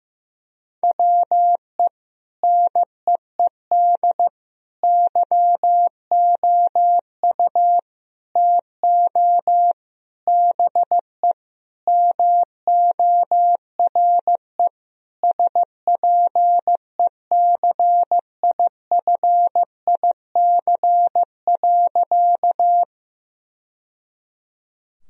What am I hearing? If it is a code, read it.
Morse code, heard as a keyed tone, about 15 wpm, 700 Hz. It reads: WE NEED YOU TO BE MORE SPECIFIC.
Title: Sentences from Top 1000 Words - Review Part 3 - 15wpm